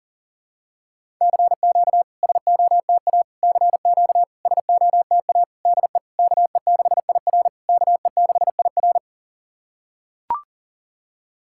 Morse code, heard as a keyed tone, about 40 wpm, 690 Hz. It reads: CQ SOTA CQ SOTA DE KE6IR KE6IR
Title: SOTA QSO Practice 40wpm